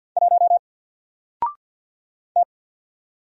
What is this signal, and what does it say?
Morse code, heard as a keyed tone, about 50 wpm, 700 Hz. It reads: 1 T
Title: Single Letter-Number 50wpm